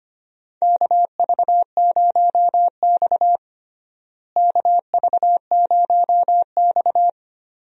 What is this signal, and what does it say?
Morse code, heard as a keyed tone, about 25 wpm, 700 Hz. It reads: KV0X KV0X
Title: USA Call Signs - 4 Characters - Repeated 2x - 25wpm